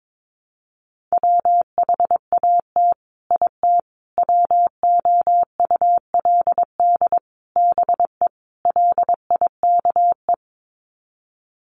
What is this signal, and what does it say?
Morse code, heard as a keyed tone, about 22 wpm, 700 Hz. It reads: WHAT IT WOULD BE LIKE
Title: Sets of 5 Words Encore 22wpm